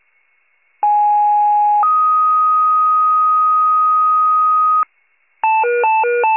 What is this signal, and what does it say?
Two-tone page, heard as a one-second tone, then a three-second tone, about 830 then 1250 Hz. It is heard from 0.8 to 4.8 s.
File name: wcalert